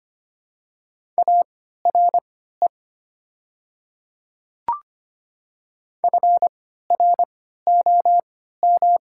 Morse code, heard as a keyed tone, about 25 wpm, spaced as for 14 wpm, 700 Hz. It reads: ARE FROM
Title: 3x Character Spacing - Top 100 Words - 25wpm